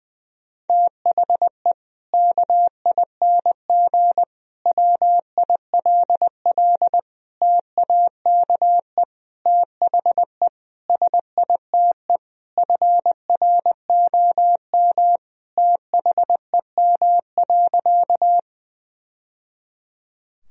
Morse code, heard as a keyed tone, about 21 wpm, 700 Hz. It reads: THE KING WILL TAKE THE SITE FROM THEM.
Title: Sentences from Top 1000 Words - Review Part 4 - 20wpm